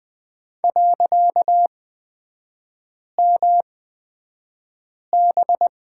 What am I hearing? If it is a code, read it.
Morse code, heard as a keyed tone, about 20 wpm, spaced as for 5 wpm, 700 Hz. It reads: .MB
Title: 255 Lesson 41e - Period - Three Characters 20wpm